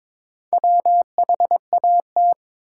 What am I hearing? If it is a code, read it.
Morse code, heard as a keyed tone, about 22 wpm, 700 Hz. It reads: WHAT
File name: Build A Phrase - 2-Word Phrases - 22wpm